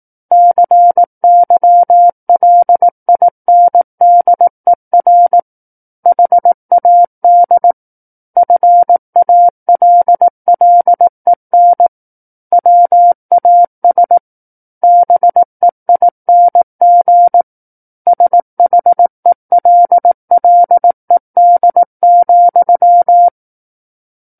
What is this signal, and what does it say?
Morse code, heard as a keyed tone, about 18 wpm, 700 Hz. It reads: CYLINDER HAD FALLEN WAS BEING SHELLED,
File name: 18 WPM